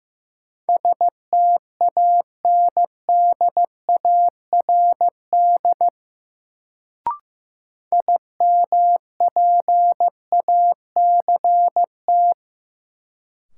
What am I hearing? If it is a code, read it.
Morse code, heard as a keyed tone, about 15 wpm, 700 Hz. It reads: STANDARD IMPACT
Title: Sentences from Top 1000 Words - Review Part 6 - New Word Drill - 15wpm